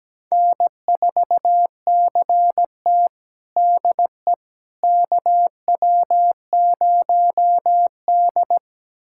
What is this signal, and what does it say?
Morse code, heard as a keyed tone, about 17 wpm, 700 Hz. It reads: N4CT DE KW0D
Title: USA Callsign DE Callsign - 4 Characters - 17wpm